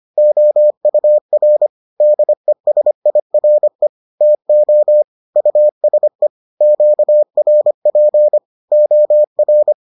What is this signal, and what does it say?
Morse code, heard as a keyed tone, about 25 wpm, 600 Hz. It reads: OUR DESIRE TO USE QRP OR